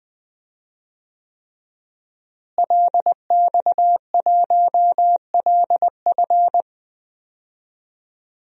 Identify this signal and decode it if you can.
Morse code, heard as a keyed tone, about 20 wpm, 700 Hz. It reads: LX1LF
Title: DXCC-100 Call Signs - 5 Characters - Repeated 3x - 20wpm